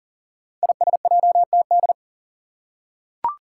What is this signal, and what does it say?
Morse code, heard as a keyed tone, about 39 wpm, 700 Hz. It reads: ISJTD
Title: Five Challenging Letter-Number Combinations 40wpm